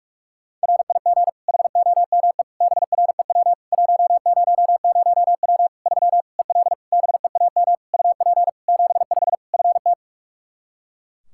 Morse code, heard as a keyed tone, about 48 wpm, 700 Hz. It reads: RIG HOME BREW 100W 3 EL BEAM UP 75 FT
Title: K1USN SST QSO Practice 45wpm